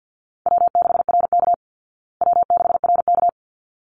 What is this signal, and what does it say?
Morse code, heard as a keyed tone, about 50 wpm, 700 Hz. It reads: W6RK W6RK
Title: Easy Call Signs Repeated 2x - 50wpm